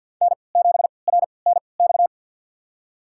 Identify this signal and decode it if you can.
Morse code, heard as a keyed tone, about 52 wpm, spaced as for 28 wpm, 700 Hz. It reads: N7RNX
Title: USA Call Signs - 5 Characters - 3x Character Spacing - 50wpm